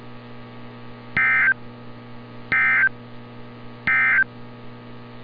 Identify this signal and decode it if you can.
Emergency Alert System (SAME) tones, heard as harsh buzzing bursts. It is heard from 1.1 to 4.3 s.